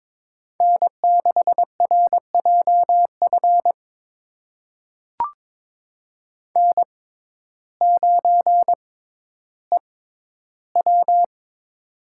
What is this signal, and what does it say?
Morse code, heard as a keyed tone, about 22 wpm, spaced as for 12 wpm, 700 Hz. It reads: N6RJF N9EW
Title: USA Call Signs - 6x Character Spacing - 22wpm